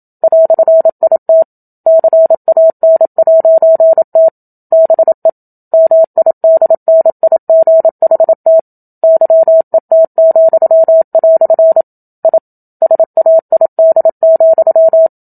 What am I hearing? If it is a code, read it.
Morse code, heard as a keyed tone, about 27 wpm, 650 Hz. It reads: "IT CAN'T BE MIDNIGHT YET," I SAID,
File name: Book Two / 01-Chapter 1 - 27 WPM